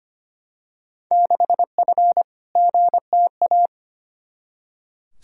Morse code, heard as a keyed tone, about 25 wpm, 700 Hz. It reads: 6F GTA
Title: Field Day Exchanges Encore - 25wpm